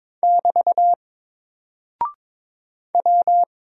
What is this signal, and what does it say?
Morse code, heard as a keyed tone, about 21 wpm, 700 Hz. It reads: <BT> W
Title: Single Character (Letter:Number:Punctuation:Prosign) Alt 22wpm